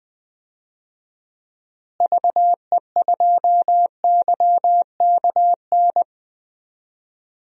Morse code, heard as a keyed tone, about 20 wpm, 700 Hz. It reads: VE2YKN
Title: DXCC-100 Call Signs - 6 Characters - Repeated 3x - 20wpm